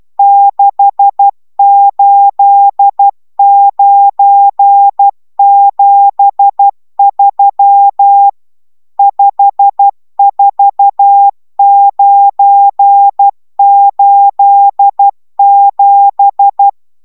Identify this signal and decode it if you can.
Morse code, heard as a keyed tone, about 12 wpm, 800 Hz. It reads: 68973 54987